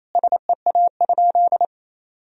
Morse code, heard as a keyed tone, about 28 wpm, 700 Hz. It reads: SEA?
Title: Sentences from Top 300 Words and Abbreviations 28wpm